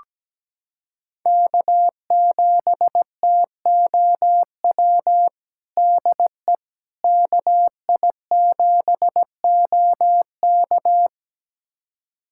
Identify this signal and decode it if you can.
Morse code, heard as a keyed tone, about 17 wpm, 700 Hz. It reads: K7TOW DE KI7OK
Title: USA Callsign DE Callsign - 5 Characters - 17wpm